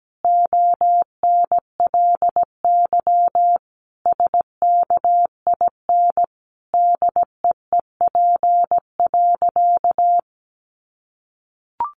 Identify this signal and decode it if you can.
Morse code, heard as a keyed tone, about 17 wpm, 700 Hz. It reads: ONLY SKIN DEEP.
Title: Sentences from Top 1300 Words 17wpm